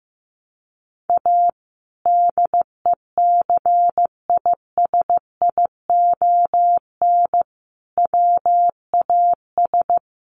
Morse code, heard as a keyed tone, about 15 wpm, 700 Hz. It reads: A DECISION WAS
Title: Sentences from Top 1100 Words 15wpm